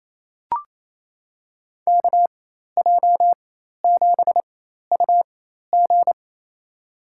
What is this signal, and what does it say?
Morse code, heard as a keyed tone, about 28 wpm, spaced as for 13 wpm, 700 Hz. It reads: KJ7UG
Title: USA Call Signs - 5 Characters - 4x Character Spacing - 28wpm